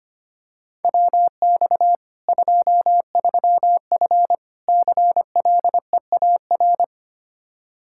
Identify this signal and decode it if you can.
Morse code, heard as a keyed tone, about 25 wpm, 700 Hz. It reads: WX 23F CLEAR
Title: K1USN SST QSO Practice 25wpm